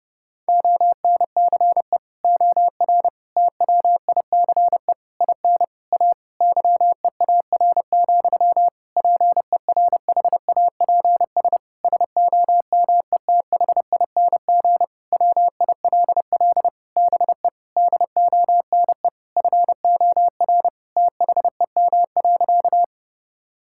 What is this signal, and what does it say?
Morse code, heard as a keyed tone, about 32 wpm, 700 Hz. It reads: ONCE OR TWICE IN A YEAR, PERHAPS SOMETHING WILL BE DONE FOR THEM.
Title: Sentences from Top 1600 Words 30wpm